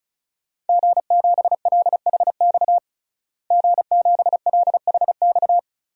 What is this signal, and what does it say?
Morse code, heard as a keyed tone, about 34 wpm, 700 Hz. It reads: G7LHX G7LHX
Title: DXCC-100 Call Signs - 5 Characters - Repeated 3x - 35wpm